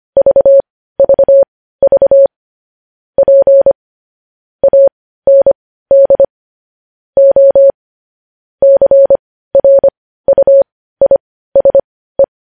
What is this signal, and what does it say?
Morse code, heard as a keyed tone, about 25 wpm, spaced as for 15 wpm, 550 Hz. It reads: VVV P AND O CRUISE
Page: News headlines in morse code at 15 wpm